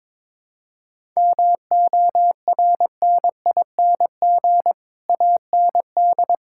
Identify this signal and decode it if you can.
Morse code, heard as a keyed tone, about 22 wpm, 700 Hz. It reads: MORNING AND
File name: Sets of 9 Words - 22wpm